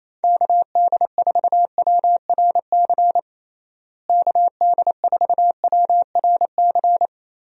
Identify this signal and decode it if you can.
Morse code, heard as a keyed tone, about 28 wpm, 700 Hz. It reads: KD4WRC KD4WRC
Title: USA Call Signs - 6 Characters - Repeated 3x - 28wpm